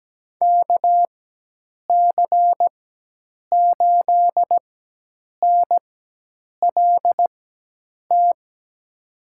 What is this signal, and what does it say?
Morse code, heard as a keyed tone, about 17 wpm, 700 Hz. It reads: K C 8 N L T
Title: USA Call Signs - 6 Characters - 4x Character Spacing - 17wpm